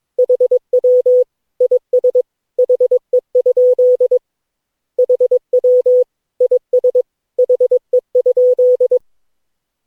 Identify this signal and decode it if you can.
Morse code, heard as a keyed tone, about 22 wpm, 500 Hz. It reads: HW IS HE? HW IS HE?
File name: cw / Session9 cw training